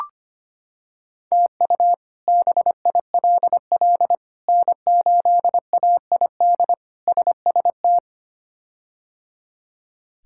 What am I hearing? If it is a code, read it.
Morse code, heard as a keyed tone, about 25 wpm, 700 Hz. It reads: TU BILL N8AID SST
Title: K1USN SST QSO Practice - 25wpm